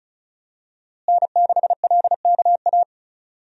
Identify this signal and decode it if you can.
Morse code, heard as a keyed tone, about 35 wpm, 700 Hz. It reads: N6LKA
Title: USA Call Signs - 5 Characters - Repeated 3x - 35wpm